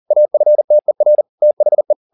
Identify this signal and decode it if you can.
Morse code, heard as a keyed tone, about 40 wpm, 600 Hz. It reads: AFTER THE